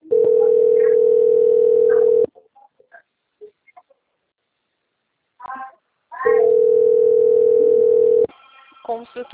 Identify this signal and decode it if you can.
Ringback tone (North America), 440 + 480 Hz, with speech.
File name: out- - - - - .97517